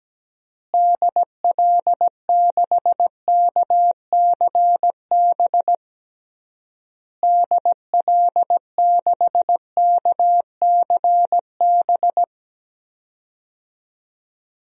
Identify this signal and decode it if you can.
Morse code, heard as a keyed tone, about 17 wpm, 700 Hz. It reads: DL6KCB DL6KCB
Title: DXCC-100 Call Signs - 6 Characters - Repeated 3x - 17wpm